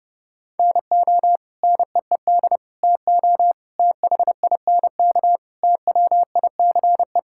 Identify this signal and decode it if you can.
Morse code, heard as a keyed tone, about 30 wpm, 700 Hz. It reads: NO NEED TO THINK TWICE
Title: Sentences from Top 1600 Words 30wpm